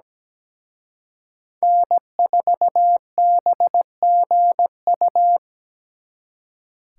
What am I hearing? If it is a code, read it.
Morse code, heard as a keyed tone, about 20 wpm, 700 Hz. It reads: N4BGU
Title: Speed Racing - USA 1x3 Call Signs 17wpm